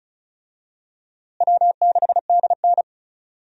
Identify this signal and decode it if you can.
Morse code, heard as a keyed tone, about 36 wpm, 700 Hz. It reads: W6DN
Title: USA Call Signs - 4 Characters - Repeated 2x - 35wpm